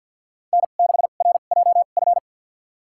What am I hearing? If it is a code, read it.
Morse code, heard as a keyed tone, about 52 wpm, spaced as for 36 wpm, 690 Hz. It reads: N6RJF
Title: USA Call Signs - 2x Character Spacing - 50wpm